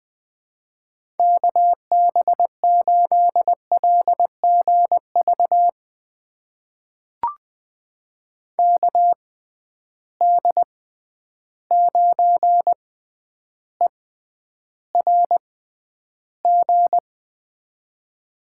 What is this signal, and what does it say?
Morse code, heard as a keyed tone, about 20 wpm, spaced as for 10 wpm, 700 Hz. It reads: KB8LGV KD9ERG
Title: USA Call Signs - 6 Characters - 6x Character Spacing - 20wpm